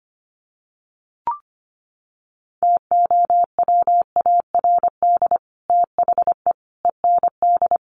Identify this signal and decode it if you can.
Morse code, heard as a keyed tone, about 25 wpm, 700 Hz. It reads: TOWARD THE END
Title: Sets of 5 Words 25wpm